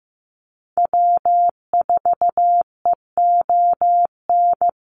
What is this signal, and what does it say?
Morse code, heard as a keyed tone, about 15 wpm, 700 Hz. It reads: W4EON
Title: NA SOTA Activator Callsigns 15wpm